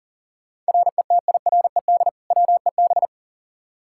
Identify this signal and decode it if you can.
Morse code, heard as a keyed tone, about 37 wpm, 700 Hz. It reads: RETIRED WEB
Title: K1USN SST QSO Practice 40wpm